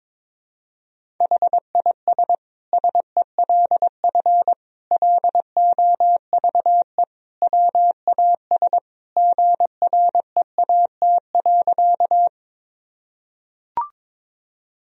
Morse code, heard as a keyed tone, about 21 wpm, 700 Hz. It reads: HIS SELF LOVE WAS GREAT.
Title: Sentences from Top 1000 Words - Review Part 1 - 22wpm